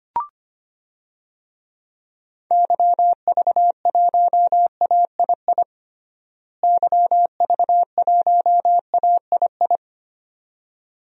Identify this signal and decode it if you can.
Morse code, heard as a keyed tone, about 25 wpm, 700 Hz. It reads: YV1AII YV1AII
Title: DXCC-100 Call Signs - 6 Characters - Repeated 2x - 25wpm